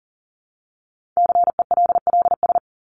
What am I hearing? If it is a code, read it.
Morse code, heard as a keyed tone, about 40 wpm, 700 Hz. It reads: CELLS
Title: ES:S Suffix - Top 100 Words with Base Words 40wpm